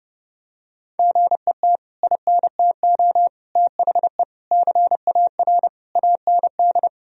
Morse code, heard as a keyed tone, about 32 wpm, 700 Hz. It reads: GET INTO THE CAR AND
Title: Sentences from Top 1000 Words - Review Part 1 - 30wpm